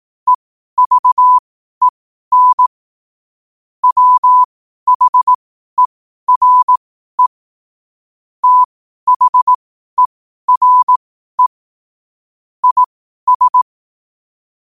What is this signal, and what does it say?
Morse code, heard as a keyed tone, about 18 wpm, spaced as for 12 wpm, 1000 Hz. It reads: EVEN WHERE THERE IS
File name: qotdfiles / 12wpm